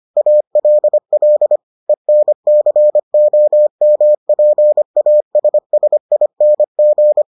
Morse code, heard as a keyed tone, about 25 wpm, 600 Hz. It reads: ALL ENCOMPASSING